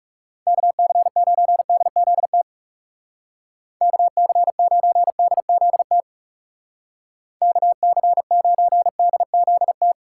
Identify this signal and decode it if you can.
Morse code, heard as a keyed tone, about 37 wpm, 700 Hz. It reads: KC9DZT KC9DZT KC9DZT
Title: Speed Racing - USA 2x3 Call Signs 30wpm